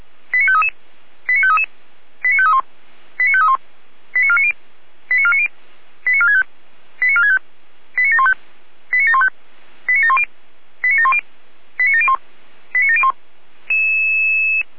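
Five-tone selective call (ZVEI), heard as quick runs of five tones, tones of about 80 ms. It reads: 89533 89533 89531 89531 89490 89490 89465 89465 89715 89715 89710 89710 89701 89701
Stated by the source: ZVEI five-tone selective call received on an SDR.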